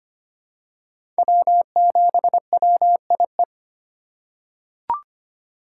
Morse code, heard as a keyed tone, about 25 wpm, 700 Hz. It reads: W7WIE
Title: USA Call Signs - 4x Character Spacing - 25wpm